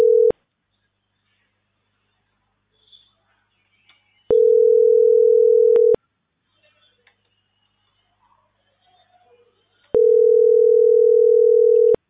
Ringback tone (North America), 440 + 480 Hz.